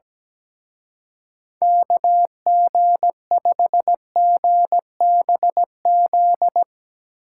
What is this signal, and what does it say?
Morse code, heard as a keyed tone, about 18 wpm, 700 Hz. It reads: KG5GBZ
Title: Speed Racing - USA 2x3 Call Signs 17wpm